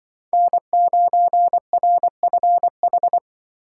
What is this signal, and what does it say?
Morse code, heard as a keyed tone, about 25 wpm, 700 Hz. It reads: N9RFH
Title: Speed Racing - USA Call Signs 20wpm